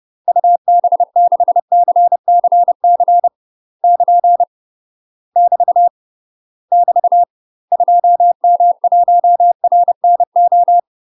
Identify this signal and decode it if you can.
Morse code, heard as a keyed tone, about 30 wpm, 700 Hz. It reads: UB6CCC <KN> <BT> <BT> 2M1RNO
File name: UK MASTER QSOs TRAINING / 30WPM / UK MASTER QSOs 0038